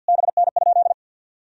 Morse code, heard as a keyed tone, about 50 wpm, 700 Hz. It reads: BN?
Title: Sentences from Top 400 Words and Abbreviations 50wpm